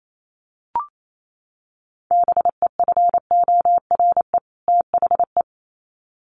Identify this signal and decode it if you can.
Morse code, heard as a keyed tone, about 28 wpm, 700 Hz. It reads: BEFORE THE
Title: Sets of 2 Words 28wpm